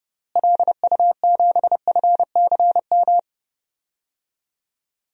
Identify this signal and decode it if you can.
Morse code, heard as a keyed tone, about 30 wpm, 700 Hz. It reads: LU7FCM
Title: DXCC-100 Call Signs - 6 Characters - Repeated 2x - 30wpm